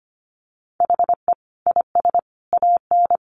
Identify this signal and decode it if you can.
Morse code, heard as a keyed tone, about 25 wpm, 700 Hz. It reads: HE IS AN
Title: Sentences from Top 1000 Words 25wpm